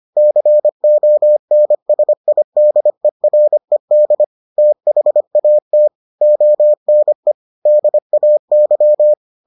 Morse code, heard as a keyed tone, about 25 wpm, 600 Hz. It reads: CONSIDERED THAT ONE DAY